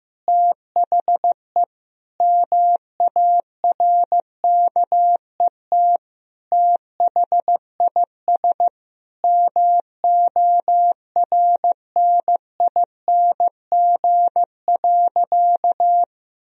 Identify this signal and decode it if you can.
Morse code, heard as a keyed tone, about 15 wpm, 700 Hz. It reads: THE MARKET THIS MORNING.
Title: Sentences from Top 1000 Words - Review Part 3 - 15wpm